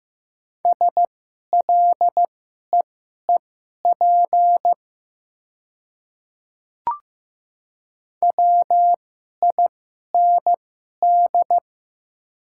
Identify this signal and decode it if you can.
Morse code, heard as a keyed tone, about 15 wpm, spaced as for 11 wpm, 700 Hz. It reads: SLEEP WIND
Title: 2x Character Spacing - Top 500 Words - 15wpm